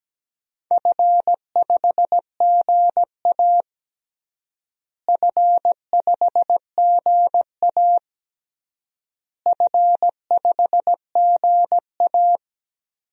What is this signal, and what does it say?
Morse code, heard as a keyed tone, about 17 wpm, 700 Hz. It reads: F5GA F5GA F5GA
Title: DXCC-100 Call Signs - 4 Characters - Repeated 3x - 17wpm